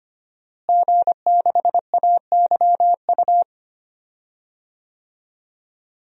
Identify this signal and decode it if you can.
Morse code, heard as a keyed tone, about 25 wpm, 700 Hz. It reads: G6AYU
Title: DXCC-100 Call Signs - 5 Characters - Repeated 3x - 25wpm